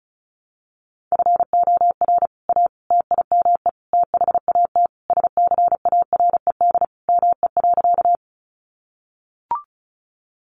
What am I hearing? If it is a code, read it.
Morse code, heard as a keyed tone, about 34 wpm, 700 Hz. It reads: FOR A TIME THAT SCARED ME.
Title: Sentences from Top 700 Words 35wpm